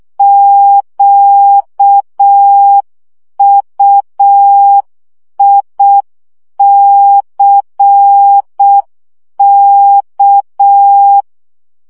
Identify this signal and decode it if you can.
Morse code, heard as a keyed tone, about 6 wpm, 800 Hz. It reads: QUICK
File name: CW 5 baud